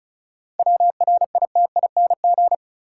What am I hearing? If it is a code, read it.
Morse code, heard as a keyed tone, about 34 wpm, 700 Hz. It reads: WRITING
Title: Sentences from Top 800 Words - New Word Drill - 35wpm